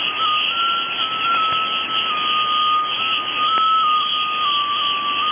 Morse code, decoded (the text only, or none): none